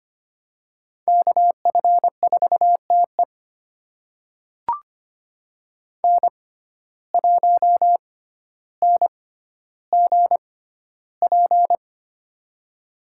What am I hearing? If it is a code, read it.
Morse code, heard as a keyed tone, about 25 wpm, spaced as for 13 wpm, 700 Hz. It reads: KF4TE N1NGP
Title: USA Call Signs - 5 Characters - 6x Character Spacing - 25wpm